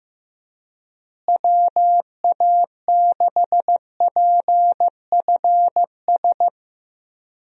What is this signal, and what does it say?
Morse code, heard as a keyed tone, about 15 wpm, 700 Hz. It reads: WA6PFS
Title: USA Call Signs - 6 Characters - Repeated 2x - 15wpm